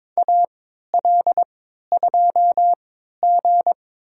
Morse code, heard as a keyed tone, about 22 wpm, spaced as for 13 wpm, 700 Hz. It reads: AL2G
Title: USA Call Signs - 4 Characters - 3x Character Spacing - 22wpm